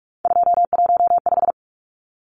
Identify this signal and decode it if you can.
Morse code, heard as a keyed tone, about 44 wpm, 700 Hz. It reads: 215
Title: Numbers - 3-Digit 45wpm